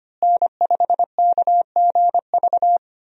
Morse code, heard as a keyed tone, about 25 wpm, 700 Hz. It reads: N5KGV
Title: USA Call Signs - 5 Characters - Repeated 2x - 25wpm